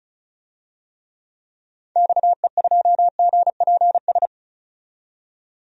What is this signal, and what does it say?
Morse code, heard as a keyed tone, about 34 wpm, 700 Hz. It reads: XE2GPS
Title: DXCC-100 Call Signs - 6 Characters - 35wpm